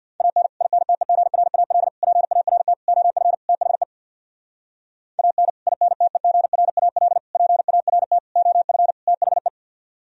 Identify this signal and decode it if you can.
Morse code, heard as a keyed tone, about 64 wpm, 700 Hz. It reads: AN INTEGRAL PART OF THE AN INTEGRAL PART OF THE